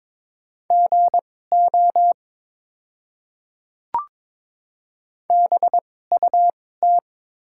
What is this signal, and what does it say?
Morse code, heard as a keyed tone, about 22 wpm, spaced as for 16 wpm, 700 Hz. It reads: GO BUT
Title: 2x Character Spacing - Top 100 Words - 22wpm